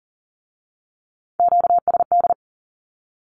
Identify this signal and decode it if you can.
Morse code, heard as a keyed tone, about 42 wpm, 700 Hz. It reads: QSD
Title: Q-Codes with Spoken Meaning - 40wpm